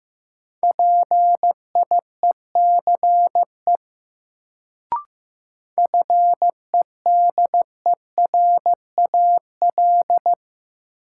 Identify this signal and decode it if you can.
Morse code, heard as a keyed tone, about 15 wpm, 700 Hz. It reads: PIECE FEDERAL
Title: Sets of 3 Words - New Word Drill - 15wpm